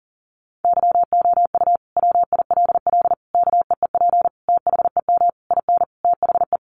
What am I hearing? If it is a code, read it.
Morse code, heard as a keyed tone, about 40 wpm, 700 Hz. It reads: YOU WILL KEEP THEM IN THE
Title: Sentences from Top 1300 Words 40wpm